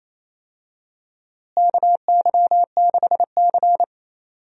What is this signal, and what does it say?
Morse code, heard as a keyed tone, about 28 wpm, 700 Hz. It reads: KY6C
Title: USA Call Signs - 4 Characters - Repeated 2x - 28wpm